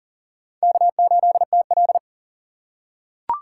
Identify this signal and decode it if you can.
Morse code, heard as a keyed tone, about 39 wpm, 700 Hz. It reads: K8TL
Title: USA Call Signs - 4 Characters - 4x Character Spacing - 40wpm